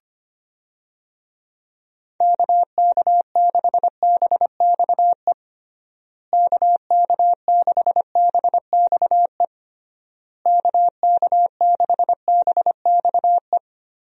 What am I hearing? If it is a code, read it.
Morse code, heard as a keyed tone, about 25 wpm, 700 Hz. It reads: KK6BXE KK6BXE KK6BXE
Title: USA Call Signs - 6 Characters - Repeated 3x - 25wpm